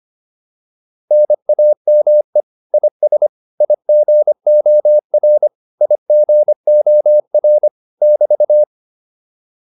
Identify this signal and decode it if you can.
Morse code, heard as a keyed tone, about 25 wpm, 600 Hz. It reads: NAME IS IGOR IGOR <BT>